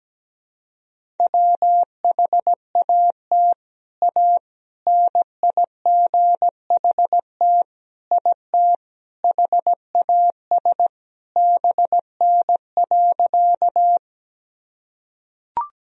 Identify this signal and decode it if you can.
Morse code, heard as a keyed tone, about 17 wpm, 700 Hz. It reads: WHAT A NIGHT IT HAS BN.
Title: Sentences from Top 300 Words and Abbreviations 17wpm